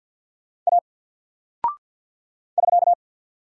Morse code, heard as a keyed tone, about 50 wpm, 700 Hz. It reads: A <SK>
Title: Single Character (Letter:Number:Punctuation:Prosign) 50wpm